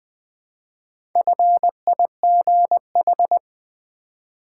Morse code, heard as a keyed tone, about 20 wpm, 700 Hz. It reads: FIGH
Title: Build A Word - 1 Letter At A Time - 5-Letter Words - 20wpm